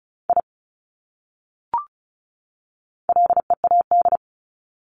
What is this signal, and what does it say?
Morse code, heard as a keyed tone, about 34 wpm, 700 Hz. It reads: I LEAD
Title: Top 500 Words 35wpm